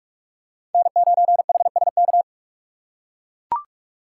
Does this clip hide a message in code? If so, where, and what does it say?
Yes: Morse code from 0.7 to 2.2 s, heard as a keyed tone, about 42 wpm, 700 Hz. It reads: N9HSK